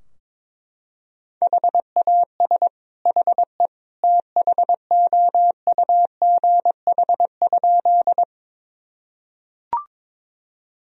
Morse code, heard as a keyed tone, about 22 wpm, 700 Hz. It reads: HAS HE THOUGH?
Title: Sentences from Top 1000 Words - Review Part 2 - 22wpm